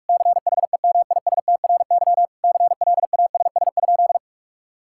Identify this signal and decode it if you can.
Morse code, heard as a keyed tone, about 47 wpm, 700 Hz. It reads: CHEMISTRY CLASS?
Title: Sentences from Top 1000 Words - Review Part 6 - 45wpm